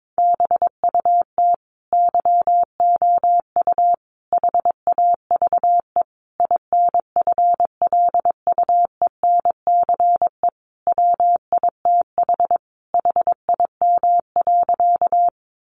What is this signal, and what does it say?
Morse code, heard as a keyed tone, about 22 wpm, 700 Hz. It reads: BUT YOU HAVE INFLUENCE WITH HIM.
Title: Sentences from Top 1100 Words 22wpm